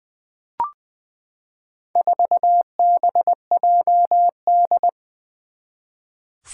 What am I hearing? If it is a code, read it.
Morse code, heard as a keyed tone, about 20 wpm, 700 Hz. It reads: 4BJD
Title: Four Challenging Letter-Number Combinations 20wpm